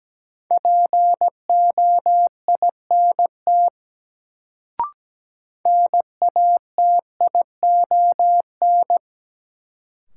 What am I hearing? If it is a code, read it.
Morse code, heard as a keyed tone, about 17 wpm, 700 Hz. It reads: POINT NATION
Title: Sets of 3 Words - New Word Drill - 17wpm